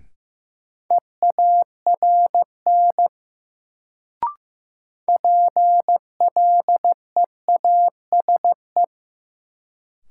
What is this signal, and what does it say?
Morse code, heard as a keyed tone, about 16 wpm, 700 Hz. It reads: EARN PLEASE
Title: Sentences from Top 1500 Words - New Word Drill Alt - 15wpm